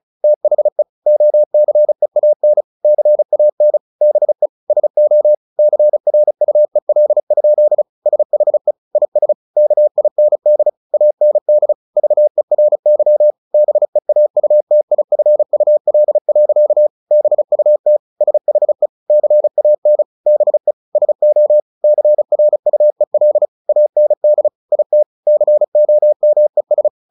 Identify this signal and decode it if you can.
Morse code, heard as a keyed tone, about 34 wpm, 600 Hz. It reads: THE OCEAN CAN BE SO CRUEL? SHE IS KIND AND VERY BEAUTIFUL. BUT SHE CAN BE SO CRUEL AND IT COMES
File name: the old man and the sea / the old man and the sea 1 35wpm